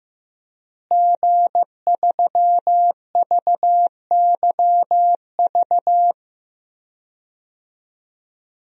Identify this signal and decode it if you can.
Morse code, heard as a keyed tone, about 15 wpm, 700 Hz. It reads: G3VYV
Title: DXCC-100 Call Signs - 5 Characters - Repeated 3x - 15wpm